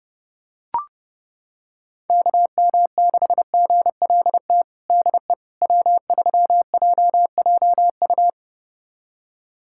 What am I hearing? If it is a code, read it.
Morse code, heard as a keyed tone, about 30 wpm, 700 Hz. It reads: KM6GLT DE W3JJU
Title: USA Callsign DE Callsign 30wpm